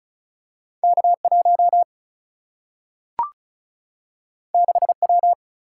Morse code, heard as a keyed tone, about 36 wpm, 700 Hz. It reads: K1 6W
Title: Single-Letter Single-Number 35wpm